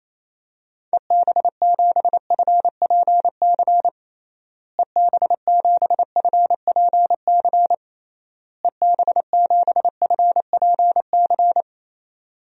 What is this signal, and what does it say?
Morse code, heard as a keyed tone, about 28 wpm, 700 Hz. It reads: EB7FPC EB7FPC EB7FPC
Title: DXCC-100 Call Signs - 6 Characters - Repeated 3x - 28wpm